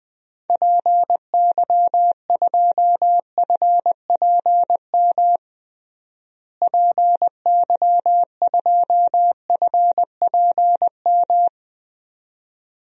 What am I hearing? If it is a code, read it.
Morse code, heard as a keyed tone, about 20 wpm, 700 Hz. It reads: PY2FPM PY2FPM
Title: DXCC-100 Call Signs - 6 Characters - Repeated 2x - 20wpm